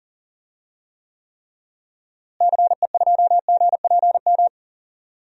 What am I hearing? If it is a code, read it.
Morse code, heard as a keyed tone, about 40 wpm, 700 Hz. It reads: CE2GPM
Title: DXCC-100 Call Signs - 6 Characters - Repeated 3x - 40wpm